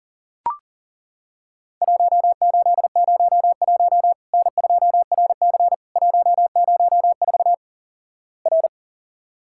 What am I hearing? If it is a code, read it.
Morse code, heard as a keyed tone, about 37 wpm, 700 Hz. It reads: 1801 N2RC 104 R
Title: Worked All Europe DX Contest - QTC Exchange - 40wpm